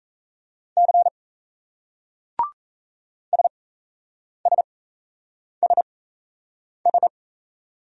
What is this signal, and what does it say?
Morse code, heard as a keyed tone, about 33 wpm, spaced as for 8 wpm, 700 Hz. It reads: C SSSS
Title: Speed-Racing Single Letters 28wpm